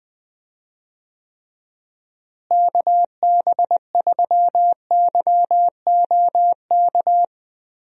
Morse code, heard as a keyed tone, about 21 wpm, 700 Hz. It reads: KB3YOK
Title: USA Call Signs - 6 Characters - Repeated 3x - 20wpm